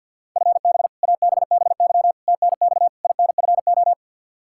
Morse code, heard as a keyed tone, about 49 wpm, 700 Hz. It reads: FB ABBY TNX INFO
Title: Practice QSO - Exchange 3 - More Info - 50wpm